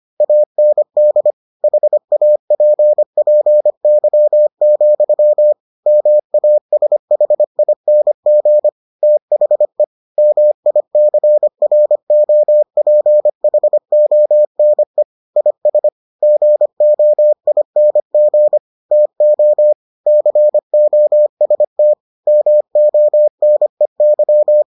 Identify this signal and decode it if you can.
Morse code, heard as a keyed tone, about 25 wpm, 600 Hz. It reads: AND HAPPY, MASHING THE MICROPHONE IS GOING TO COST MONEY